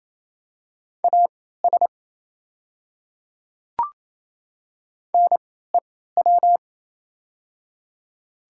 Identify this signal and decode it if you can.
Morse code, heard as a keyed tone, about 28 wpm, spaced as for 16 wpm, 700 Hz. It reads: AS NEW
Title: 3x Character Spacing - Top 300 Words - 28wpm